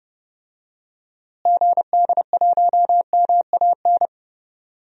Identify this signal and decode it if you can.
Morse code, heard as a keyed tone, about 32 wpm, 700 Hz. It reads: GD1MAN
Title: DXCC-100 Call Signs - 6 Characters - Repeated 3x - 30wpm